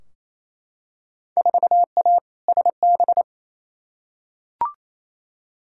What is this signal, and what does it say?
Morse code, heard as a keyed tone, about 28 wpm, 700 Hz. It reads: 4A SB
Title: Field Day Exchanges Encore - 28wpm